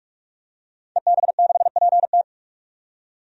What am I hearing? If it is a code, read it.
Morse code, heard as a keyed tone, about 41 wpm, 700 Hz. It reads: EB6PT
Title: DXCC-100 Call Signs - 5 Characters - 45wpm